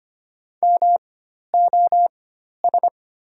Morse code, heard as a keyed tone, about 25 wpm, spaced as for 12 wpm, 700 Hz. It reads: MOS
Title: 4x Character Spacing - Top 100 Words - 25wpm